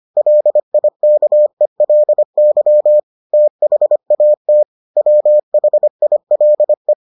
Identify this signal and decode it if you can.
Morse code, heard as a keyed tone, about 25 wpm, 600 Hz. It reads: LIKELY THAT WHILE